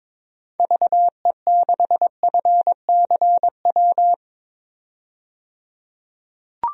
Morse code, heard as a keyed tone, about 22 wpm, 700 Hz. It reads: VE6FCW
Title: DXCC-100 Call Signs - 6 Characters - 22wpm